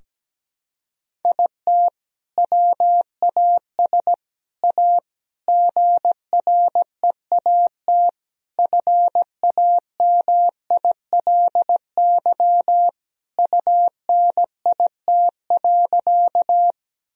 Morse code, heard as a keyed tone, about 17 wpm, 700 Hz. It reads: IT WAS A GREAT FAMILY UNIT.
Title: Sentences from Top 1000 Words - Review Part 2 - 17wpm